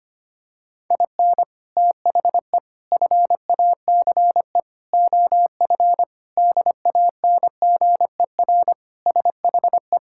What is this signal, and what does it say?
Morse code, heard as a keyed tone, about 25 wpm, 700 Hz. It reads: IN THE FACE OF DANGER SHE
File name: practice / 25wpm / Sentences with 4-Word Phrases - 25wpm